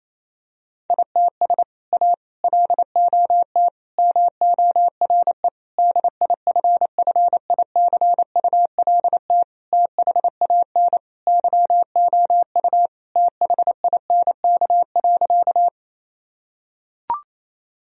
Morse code, heard as a keyed tone, about 27 wpm, 700 Hz. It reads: ITS A LOT MORE DIFFICULT THAN YOU THINK.
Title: Sentences from Top 1000 Words - Review Part 3 - 28wpm